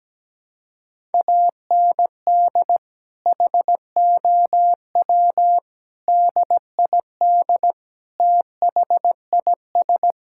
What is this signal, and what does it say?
Morse code, heard as a keyed tone, about 17 wpm, 700 Hz. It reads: AND HOW DID THIS
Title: Sentences from Top 1000 Words - Review Part 6 - 17wpm